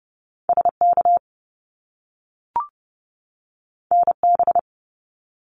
Morse code, heard as a keyed tone, about 30 wpm, 700 Hz. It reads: SK NB
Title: Canadian Provinces and Territories 30wpm